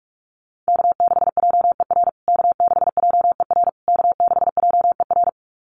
Morse code, heard as a keyed tone, about 44 wpm, 700 Hz. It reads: K6JER K6JER K6JER
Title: USA Call Signs Repeated 3x 45wpm